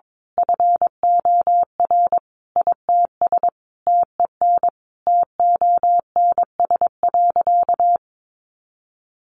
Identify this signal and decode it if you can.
Morse code, heard as a keyed tone, about 22 wpm, 700 Hz. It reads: FOR ITS TEN TONS.
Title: A Sound Of Thunder 22wpm